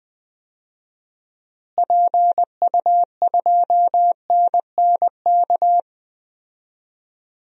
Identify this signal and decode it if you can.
Morse code, heard as a keyed tone, about 21 wpm, 700 Hz. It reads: PU2NNK
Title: DXCC-100 Call Signs - 6 Characters - Repeated 2x - 20wpm